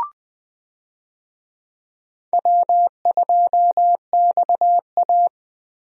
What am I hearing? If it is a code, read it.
Morse code, heard as a keyed tone, about 20 wpm, 700 Hz. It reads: W2XA
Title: USA Call Signs - 4 Characters - Repeated 2x - 20wpm